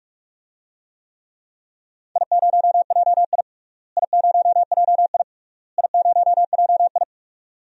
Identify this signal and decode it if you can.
Morse code, heard as a keyed tone, about 46 wpm, 700 Hz. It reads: I0JI I0JI I0JI
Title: DXCC-100 Call Signs - 4 Characters - Repeated 3x - 45wpm